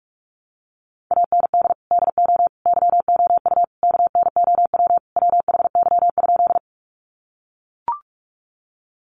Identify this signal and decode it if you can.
Morse code, heard as a keyed tone, about 44 wpm, 700 Hz. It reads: AND DO YOU KNOW WHY?
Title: Sentences from Top 200 Words 45wpm